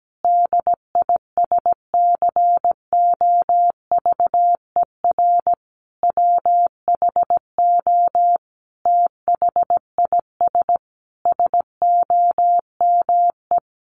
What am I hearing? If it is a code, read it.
Morse code, heard as a keyed tone, about 17 wpm, 700 Hz. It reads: DISCOVER WHO THIS SOME
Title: Sentences from Top 1100 Words 17wpm